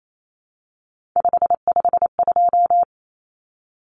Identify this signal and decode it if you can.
Morse code, heard as a keyed tone, about 28 wpm, 700 Hz. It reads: 552
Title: Numbers - 3-Digit 28wpm